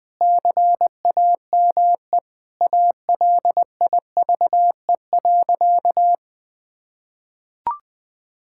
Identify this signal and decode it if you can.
Morse code, heard as a keyed tone, about 20 wpm, 700 Hz. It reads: CAME ALIVE.
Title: Sentences with 5-Word Phrases - Spoken Beginning - 20wpm